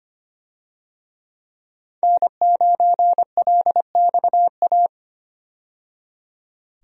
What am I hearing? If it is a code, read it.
Morse code, heard as a keyed tone, about 25 wpm, 700 Hz. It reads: N9LXA
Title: USA Call Signs - 5 Characters - 25wpm